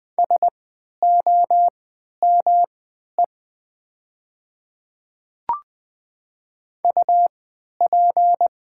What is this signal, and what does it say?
Morse code, heard as a keyed tone, about 20 wpm, spaced as for 11 wpm, 700 Hz. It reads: SOME UP
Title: 3x Character Spacing - Top 100 Words - 20wpm